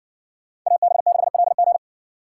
Morse code, heard as a keyed tone, about 58 wpm, 700 Hz. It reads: AB6LG